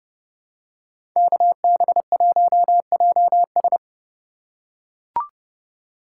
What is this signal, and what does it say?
Morse code, heard as a keyed tone, about 30 wpm, 700 Hz. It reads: KB1JS
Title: USA Call Signs - 5 Characters - 4x Character Spacing - 30wpm